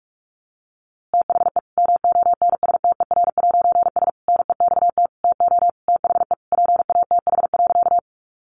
Morse code, heard as a keyed tone, about 46 wpm, 700 Hz. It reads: THE MONSTER'S NEXT TO THE PATH.
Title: A Sound Of Thunder 45wpm